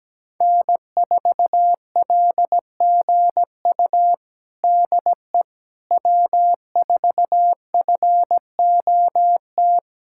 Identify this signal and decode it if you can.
Morse code, heard as a keyed tone, about 17 wpm, 700 Hz. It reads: N4LGU DE W4FOT
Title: USA Callsign DE Callsign - 5 Characters - 17wpm